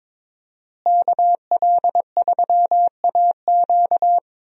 Morse code, heard as a keyed tone, about 22 wpm, 700 Hz. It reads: KL3AQ
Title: USA Call Signs - 5 Characters - Repeated 3x - 22wpm